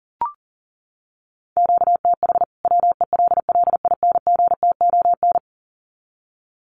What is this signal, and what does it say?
Morse code, heard as a keyed tone, about 40 wpm, 700 Hz. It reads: QTH WELLINGTON
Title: Common QSO Phrases 40wpm